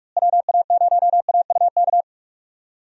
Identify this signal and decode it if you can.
Morse code, heard as a keyed tone, about 41 wpm, 700 Hz. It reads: WA0AUK
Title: USA Call Signs - 6 Characters - Repeated 2x - 45wpm